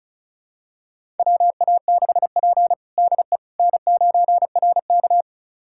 Morse code, heard as a keyed tone, about 36 wpm, 700 Hz. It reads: WA6P DE N9RK
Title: USA Callsign DE Callsign - 4 Characters - 35wpm